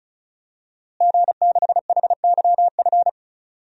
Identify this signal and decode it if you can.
Morse code, heard as a keyed tone, about 36 wpm, 700 Hz. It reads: G6HYF
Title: DXCC-100 Call Signs - 5 Characters - Repeated 3x - 35wpm